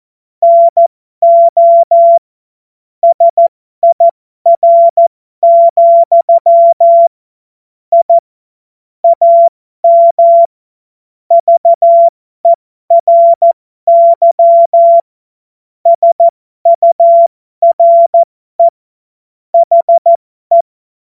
Morse code, heard as a keyed tone, about 14 wpm, spaced as for 12 wpm, 680 Hz. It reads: NO SIR, I AM VERY SURE HE
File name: Morsekurs / Witze